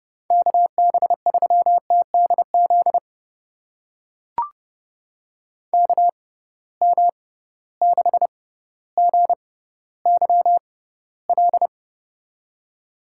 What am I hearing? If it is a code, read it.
Morse code, heard as a keyed tone, about 31 wpm, spaced as for 15 wpm, 700 Hz. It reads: KB3TDZ KM6GYL
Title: USA Call Signs - 6 Characters - 6x Character Spacing - 30wpm